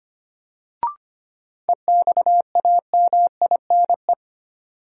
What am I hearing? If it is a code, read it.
Morse code, heard as a keyed tone, about 25 wpm, 700 Hz. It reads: EXAMINE
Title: Sentences from Top 1600 Words - New Word Drill Alt - 25wpm